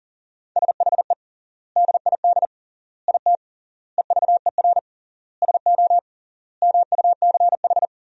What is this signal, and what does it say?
Morse code, heard as a keyed tone, about 37 wpm, 700 Hz. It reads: SHE DID IT EVER SO MUCH
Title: 3x Word-Spacing - Sentences from Top 400 Words - 40wpm